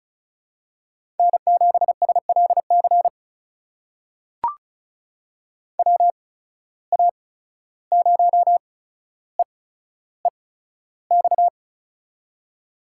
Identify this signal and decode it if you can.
Morse code, heard as a keyed tone, about 36 wpm, spaced as for 14 wpm, 700 Hz. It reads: N7SLC WA0EEX
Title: USA Call Signs - 8x Character Spacing - 35wpm